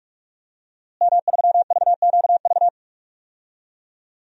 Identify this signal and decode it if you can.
Morse code, heard as a keyed tone, about 46 wpm, 700 Hz. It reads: M3VQV
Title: DXCC-100 Call Signs - 5 Characters - Repeated 3x - 45wpm